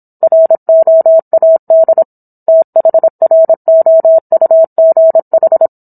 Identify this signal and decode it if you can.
Morse code, heard as a keyed tone, about 27 wpm, 650 Hz. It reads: ROAD THROUGH
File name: Book One / 17-Chapter 17 - 26 WPM